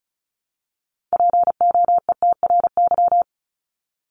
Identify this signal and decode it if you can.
Morse code, heard as a keyed tone, about 34 wpm, 700 Hz. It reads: POETRY
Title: Top 1751-2000 Words 35wpm